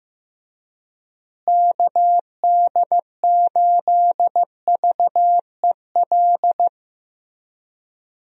Morse code, heard as a keyed tone, about 15 wpm, 700 Hz. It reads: KD8VEL
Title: USA Call Signs - 6 Characters - Repeated 3x - 15wpm